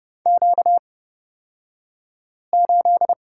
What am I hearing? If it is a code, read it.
Morse code, heard as a keyed tone, about 30 wpm, spaced as for 5 wpm, 700 Hz. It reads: Q8
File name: Lesson 38d - Letter q - Two Characters 30wpm